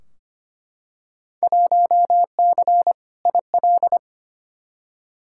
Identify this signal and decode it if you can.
Morse code, heard as a keyed tone, about 25 wpm, 700 Hz. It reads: 1C IL
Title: Field Day Exchanges - 25wpm